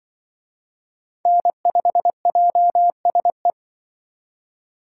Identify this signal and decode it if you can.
Morse code, heard as a keyed tone, about 24 wpm, 700 Hz. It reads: N5JSE
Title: Speed Racing - USA 1x3 Call Signs 20wpm